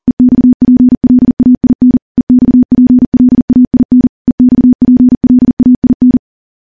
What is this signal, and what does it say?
Morse code, heard as a keyed tone, about 42 wpm, 250 Hz. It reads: EXPLAIN EXPLAIN EXPLAIN
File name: morse / morse-training-40WPM